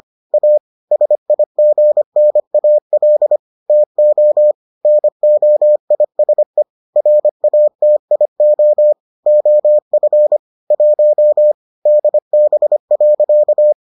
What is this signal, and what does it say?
Morse code, heard as a keyed tone, about 25 wpm, 600 Hz. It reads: A SIGNAL TO NOISE RATIO OF 1 DB.